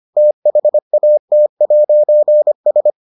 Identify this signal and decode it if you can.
Morse code, heard as a keyed tone, about 25 wpm, 600 Hz. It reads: THAT'S